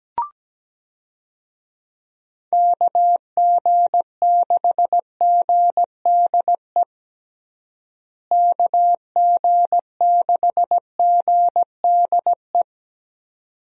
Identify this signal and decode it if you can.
Morse code, heard as a keyed tone, about 17 wpm, 700 Hz. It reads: KG6GDE KG6GDE
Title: USA Call Signs - 6 Characters - Repeated 2x - 17wpm